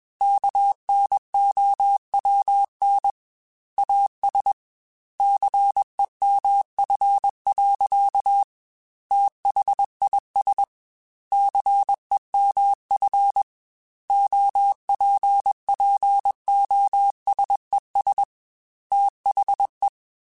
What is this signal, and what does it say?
Morse code, heard as a keyed tone, about 21 wpm, 800 Hz. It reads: KNOWN AS CEMF. THIS CEMF OPPOSES THE